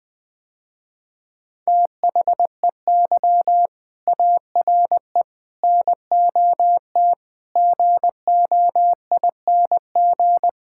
Morse code, heard as a keyed tone, about 21 wpm, 700 Hz. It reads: THEY ARE NOT GOING
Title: Sentences with 4-Word Phrases - Spoken Beginning - 20wpm